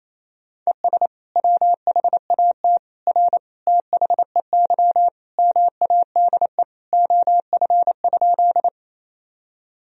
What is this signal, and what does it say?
Morse code, heard as a keyed tone, about 28 wpm, 700 Hz. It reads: ES WHAT R THEY MADE OF?
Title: Sentences from Top 200 Words and Abbreviations 28wpm